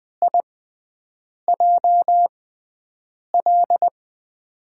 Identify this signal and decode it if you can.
Morse code, heard as a keyed tone, about 20 wpm, spaced as for 7 wpm, 700 Hz. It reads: IJL
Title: USA Call Signs - 5 Characters - 6x Character Spacing - 20wpm